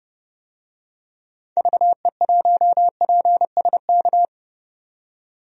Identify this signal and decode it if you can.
Morse code, heard as a keyed tone, about 31 wpm, 700 Hz. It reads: VE1PSK
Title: DXCC-100 Call Signs - 6 Characters - 30wpm